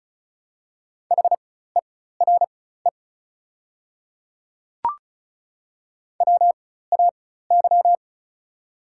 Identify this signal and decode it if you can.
Morse code, heard as a keyed tone, about 36 wpm, spaced as for 16 wpm, 690 Hz. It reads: HERE WAY
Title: 4x Character Spacing - Top 300 Words - 35wpm